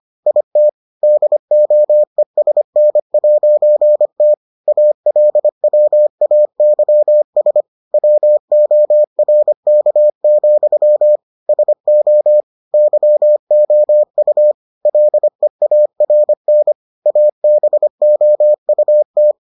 Morse code, heard as a keyed tone, about 25 wpm, 600 Hz. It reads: IT DOESN'T ALWAYS WORK, SO YOU LEARN ABOUT